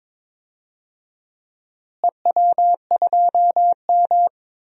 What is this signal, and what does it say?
Morse code, heard as a keyed tone, about 22 wpm, 700 Hz. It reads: EW2M
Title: DXCC-100 Call Signs - 4 Characters - Repeated 2x - 22wpm